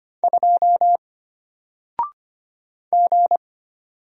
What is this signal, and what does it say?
Morse code, heard as a keyed tone, about 25 wpm, 700 Hz. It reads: 2 G
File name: Single-Letter-Number-25wpm